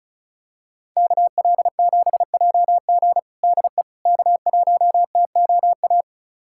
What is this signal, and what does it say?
Morse code, heard as a keyed tone, about 36 wpm, 700 Hz. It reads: KL7JG DE K1TOA
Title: USA Callsign DE Callsign - 5 Characters - 35wpm